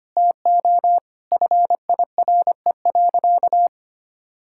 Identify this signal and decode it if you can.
Morse code, heard as a keyed tone, about 25 wpm, 700 Hz. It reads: TO FIRE.
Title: Sentences from Top 1000 Words - Review Part 6 - 25wpm